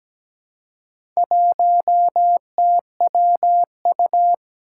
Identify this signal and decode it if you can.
Morse code, heard as a keyed tone, about 17 wpm, 700 Hz. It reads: 1TWU
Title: Four Challenging Letter-Number Combinations 17wpm